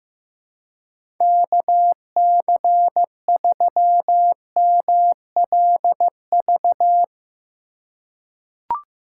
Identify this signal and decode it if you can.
Morse code, heard as a keyed tone, about 15 wpm, 700 Hz. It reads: KC3MLV
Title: USA Call Signs - 6 Characters - 3x Character Spacing - 15wpm